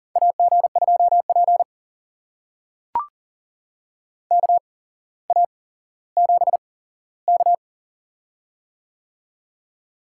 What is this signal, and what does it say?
Morse code, heard as a keyed tone, about 41 wpm, spaced as for 17 wpm, 700 Hz. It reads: AG2P KA7K
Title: USA Call Signs - 4 Characters - 8x Character Spacing - 40wpm